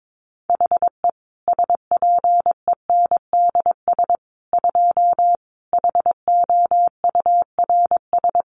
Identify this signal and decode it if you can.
Morse code, heard as a keyed tone, about 22 wpm, 700 Hz. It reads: HE SPENDS 2 HOURS
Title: Additional Easy Phrases 22wpm